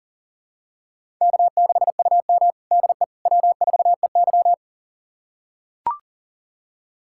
Morse code, heard as a keyed tone, about 40 wpm, 700 Hz. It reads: K6UM DE W4EY
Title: USA Callsign DE Callsign - 4 Characters - 40wpm